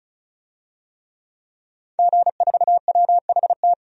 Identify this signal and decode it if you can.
Morse code, heard as a keyed tone, about 36 wpm, 700 Hz. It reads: G4WHT
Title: DXCC-100 Call Signs - 5 Characters - 35wpm